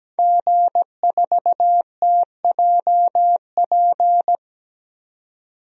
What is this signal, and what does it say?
Morse code, heard as a keyed tone, about 17 wpm, 700 Hz. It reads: G4TJP
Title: DXCC-100 Call Signs - 5 Characters - 17wpm